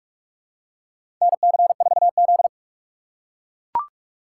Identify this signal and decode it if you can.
Morse code, heard as a keyed tone, about 44 wpm, 700 Hz. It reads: NC4Z
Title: USA Call Signs - 2x Character Spacing - 45wpm